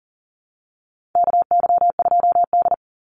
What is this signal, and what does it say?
Morse code, heard as a keyed tone, about 38 wpm, 700 Hz. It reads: KY2D
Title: Call Signs - 40wpm